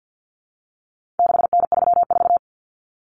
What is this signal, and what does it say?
Morse code, heard as a keyed tone, about 50 wpm, 700 Hz. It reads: 6N34
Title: Common Cut Numbers - 4-Digit 50wpm